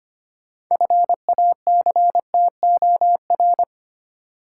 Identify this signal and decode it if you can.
Morse code, heard as a keyed tone, about 25 wpm, 700 Hz. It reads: FACTOR
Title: Six Letter Words - 25wpm